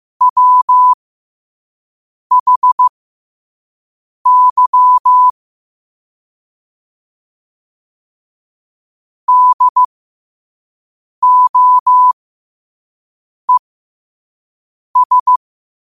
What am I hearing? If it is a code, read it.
Morse code, heard as a keyed tone, about 15 wpm, spaced as for 5 wpm, 1000 Hz. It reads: WHY DOES
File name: 5wpm